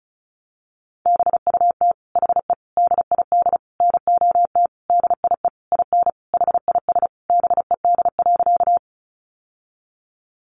Morse code, heard as a keyed tone, about 34 wpm, 700 Hz. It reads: BUT HE DID NOT DIE IN HIS BED.
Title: Sentences from Top 1300 Words 35wpm